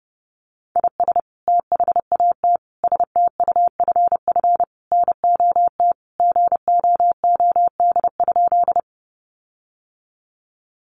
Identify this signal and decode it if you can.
Morse code, heard as a keyed tone, about 30 wpm, 700 Hz. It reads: IS THAT STUFF NOT GOOD?
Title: Sentences from Top 1400 Words 30wpm